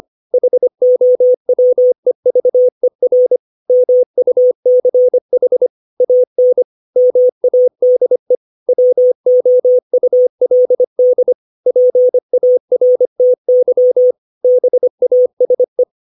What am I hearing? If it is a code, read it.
Morse code, heard as a keyed tone, about 25 wpm, 500 Hz. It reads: HOWEVER MUCH AN MADE WOULD PARTY BASE
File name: 632 words / 632 Words 25WPM